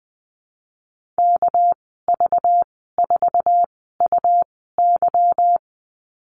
Morse code, heard as a keyed tone, about 20 wpm, spaced as for 14 wpm, 700 Hz. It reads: KV4UY
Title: 238 Lesson 38g - Letter q - Call Signs 20wpm